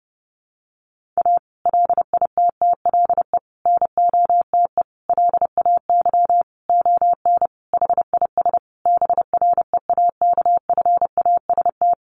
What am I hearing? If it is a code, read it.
Morse code, heard as a keyed tone, about 30 wpm, 700 Hz. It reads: A LITTLE NOTE LAY ON HIS BREAKFAST
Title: Sentences from Top 1500 Words 30wpm